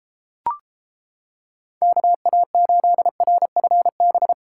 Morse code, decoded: KA8RFB

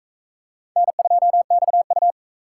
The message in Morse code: N2XU

